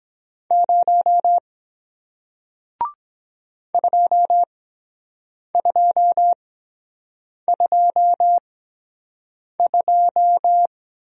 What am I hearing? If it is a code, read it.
Morse code, heard as a keyed tone, about 22 wpm, spaced as for 7 wpm, 700 Hz. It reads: 0 2222